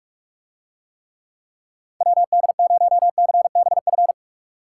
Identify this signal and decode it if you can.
Morse code, heard as a keyed tone, about 49 wpm, 700 Hz. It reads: WD0CBF